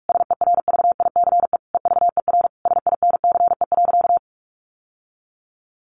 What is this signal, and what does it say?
Morse code, heard as a keyed tone, about 44 wpm, 700 Hz. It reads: SERVICE EVER SINCE.
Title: Sentences from Top 700 Words 45wpm